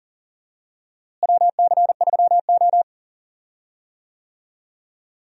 Morse code, heard as a keyed tone, about 40 wpm, 700 Hz. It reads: WC3O
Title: USA Call Signs - 4 Characters - Repeated 2x - 40wpm